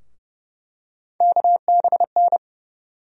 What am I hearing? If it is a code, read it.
Morse code, heard as a keyed tone, about 30 wpm, 700 Hz. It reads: KBN
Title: USA Call Sign Postfixes - 3 Letters 30wpm